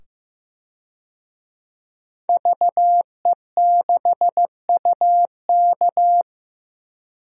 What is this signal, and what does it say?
Morse code, heard as a keyed tone, about 15 wpm, 700 Hz. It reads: VE6UK
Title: DXCC-100 Call Signs - 5 Characters - 15wpm